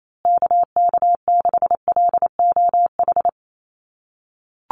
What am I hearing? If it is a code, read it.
Morse code, heard as a keyed tone, about 28 wpm, 700 Hz. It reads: KK6LOH
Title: USA Call Signs Repeated 2x 28wpm